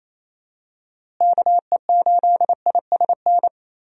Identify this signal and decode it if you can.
Morse code, heard as a keyed tone, about 28 wpm, 700 Hz. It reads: KE8ISN
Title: USA Call Signs - 6 Characters - Repeated 2x - 28wpm